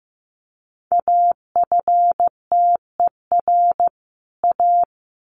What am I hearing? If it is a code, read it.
Morse code, heard as a keyed tone, about 15 wpm, 700 Hz. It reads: AFTER A
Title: Sets of 2 Words 15wpm